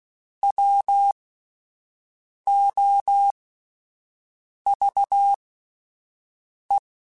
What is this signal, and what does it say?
Morse code, heard as a keyed tone, about 16 wpm, spaced as for 5 wpm, 800 Hz. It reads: WOVE
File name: Alphabetical / Alphabetical mp3s / Ws @5